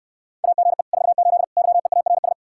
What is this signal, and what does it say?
Morse code, heard as a keyed tone, about 69 wpm, 700 Hz. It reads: AGE 48 YEARS